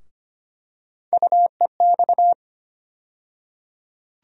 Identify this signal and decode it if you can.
Morse code, heard as a keyed tone, about 25 wpm, 700 Hz. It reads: UEX
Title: USA Call Sign Postfixes - 3 Letters 25wpm